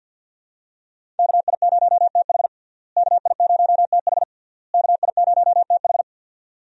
Morse code, heard as a keyed tone, about 50 wpm, 700 Hz. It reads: KI0TH KI0TH KI0TH
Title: USA Call Signs - 5 Characters - Repeated 3x - 50wpm